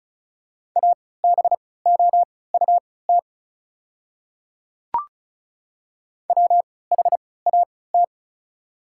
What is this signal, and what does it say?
Morse code, heard as a keyed tone, about 36 wpm, spaced as for 20 wpm, 690 Hz. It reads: ABOUT WHAT
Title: 3x Character Spacing - Top 100 Words - 35wpm